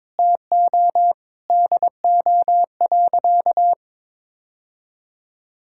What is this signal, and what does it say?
Morse code, heard as a keyed tone, about 22 wpm, 700 Hz. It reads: TO DO.
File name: practice / 22wpm / Sentences with 8-Word Phrases - 22wpm